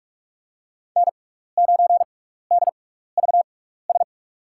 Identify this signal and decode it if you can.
Morse code, heard as a keyed tone, about 42 wpm, spaced as for 15 wpm, 700 Hz. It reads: N9DVS
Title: USA Call Signs - 5 Characters - 6x Character Spacing - 45wpm